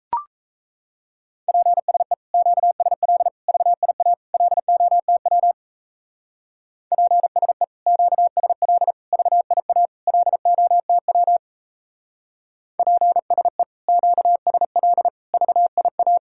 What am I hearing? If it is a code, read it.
Morse code, heard as a keyed tone, about 34 wpm, 700 Hz. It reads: PSE QSL VIA LOTW PSE QSL VIA LOTW PSE QSL VIA
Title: Speed-Racing Common QSO Phrases - 28wpm